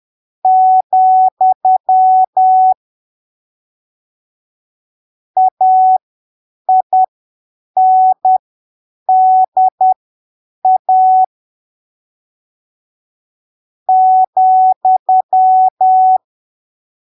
Morse code, heard as a keyed tone, about 10 wpm, spaced as for 7 wpm, 750 Hz. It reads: , AINDA ,
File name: arquivos / texto / texto 23 10ppm